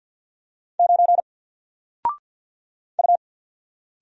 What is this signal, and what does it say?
Morse code, heard as a keyed tone, about 50 wpm, 700 Hz. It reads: 9 U
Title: Single Letter-Number 50wpm